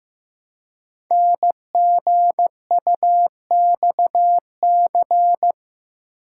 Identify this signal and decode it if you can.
Morse code, heard as a keyed tone, about 15 wpm, 700 Hz. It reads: NGUXC
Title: Code Groups - 5 Letter:Numbers - 15wpm